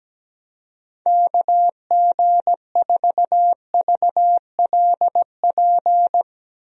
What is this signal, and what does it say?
Morse code, heard as a keyed tone, about 17 wpm, 700 Hz. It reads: KG4VLP
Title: USA Call Signs - 6 Characters - 17wpm